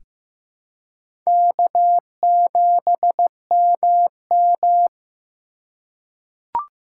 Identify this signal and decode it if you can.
Morse code, heard as a keyed tone, about 15 wpm, 700 Hz. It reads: K7MM
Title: USA Call Signs - 3x Character Spacing - 15wpm